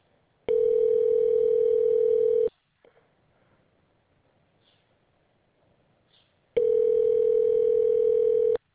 Ringback tone (North America), 440 + 480 Hz.